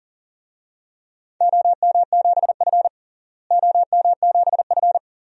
Morse code, heard as a keyed tone, about 40 wpm, 700 Hz. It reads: OM7F OM7F
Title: DXCC-100 Call Signs - 4 Characters - Repeated 2x - 40wpm